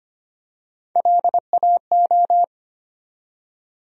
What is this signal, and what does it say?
Morse code, heard as a keyed tone, about 25 wpm, 700 Hz. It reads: LAO
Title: USA Call Sign Postfixes - 3 Letters 25wpm